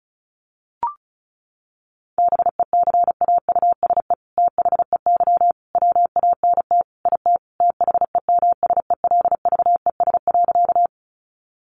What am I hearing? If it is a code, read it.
Morse code, heard as a keyed tone, about 34 wpm, 700 Hz. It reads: BECAUSE THEY WANT IT THEMSELVES.
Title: Sentences from Top 800 Words 35wpm